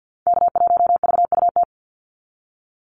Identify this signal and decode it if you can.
Morse code, heard as a keyed tone, about 50 wpm, 700 Hz. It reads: K1VUT
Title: Call Signs Repeated 2x - 50wpm